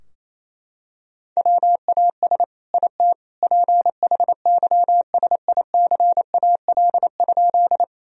Morse code, heard as a keyed tone, about 28 wpm, 700 Hz. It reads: WAS IT PHYSICAL?
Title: Sentences from Top 1000 Words - Review Part 4 - 28wpm